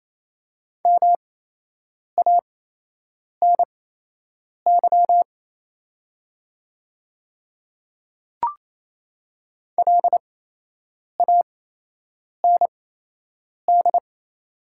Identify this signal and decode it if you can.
Morse code, heard as a keyed tone, about 28 wpm, spaced as for 8 wpm, 700 Hz. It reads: MANY LAND
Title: 8x Character Spacing - Top 200 Words - 28wpm